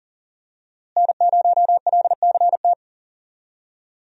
Morse code, heard as a keyed tone, about 40 wpm, 700 Hz. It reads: N0LCT
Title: USA Call Signs - 5 Characters - Repeated 2x - 40wpm